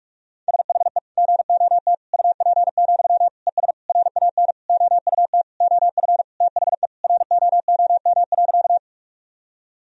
Morse code, heard as a keyed tone, about 42 wpm, 700 Hz. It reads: SHE GOT UP, ES RAN OUT OF THE ROOM.
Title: Sentences from Top 500 Words and Abbreviations 45wpm